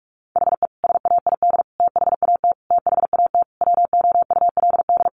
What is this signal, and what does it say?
Morse code, heard as a keyed tone, about 44 wpm, 700 Hz. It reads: HE SAID THAT THAT WOULD
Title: Sentences from Top 200 Words 45wpm